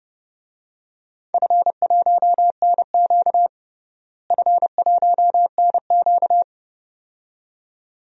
Morse code, heard as a keyed tone, about 30 wpm, 700 Hz. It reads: F1NQ F1NQ
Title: DXCC-100 Call Signs - 4 Characters - Repeated 2x - 30wpm